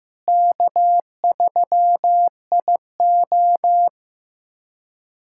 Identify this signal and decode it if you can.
Morse code, heard as a keyed tone, about 15 wpm, 700 Hz. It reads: K3IO
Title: USA Callsign DE Callsign - 4 Characters - 15wpm